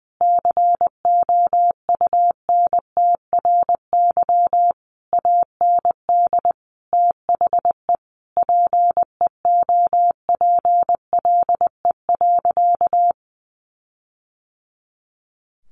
Morse code, heard as a keyed tone, about 20 wpm, 700 Hz. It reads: COUNTRY AND THE PEOPLE.
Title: Sentences from Top 1500 Words 20wpm